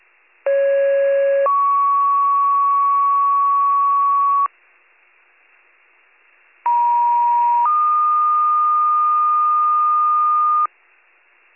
Two-tone page, heard as a one-second tone, then a three-second tone, about 550 then 1120 Hz, about 950 then 1220 Hz. It is heard from 0.5 to 10.7 s.